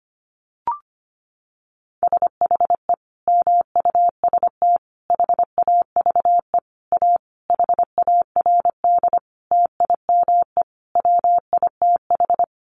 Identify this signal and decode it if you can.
Morse code, heard as a keyed tone, about 25 wpm, 700 Hz. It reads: SHE MUST HAVE A HARD TIME WITH
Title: Sentences from Top 300 Words 25wpm